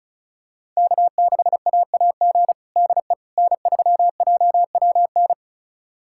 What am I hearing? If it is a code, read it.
Morse code, heard as a keyed tone, about 36 wpm, 700 Hz. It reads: K6AAG DE N3JWN
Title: USA Callsign DE Callsign - 5 Characters - 35wpm